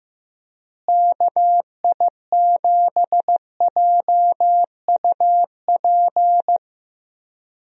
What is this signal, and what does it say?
Morse code, heard as a keyed tone, about 15 wpm, 700 Hz. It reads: KI7JUP
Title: USA Call Signs - 6 Characters - Repeated 2x - 15wpm